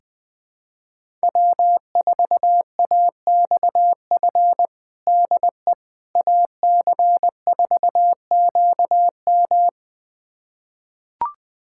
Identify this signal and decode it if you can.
Morse code, heard as a keyed tone, about 20 wpm, 700 Hz. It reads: W4AXF DE AC4QM
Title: USA Callsign DE Callsign - 5 Characters - 20wpm